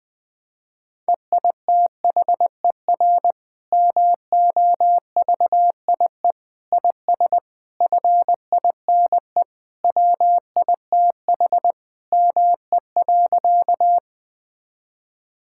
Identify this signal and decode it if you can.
Morse code, heard as a keyed tone, about 21 wpm, 700 Hz. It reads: EITHER MOVIE IS FINE WITH ME.
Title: Sentences from Top 1000 Words - Review Part 5 - 20wpm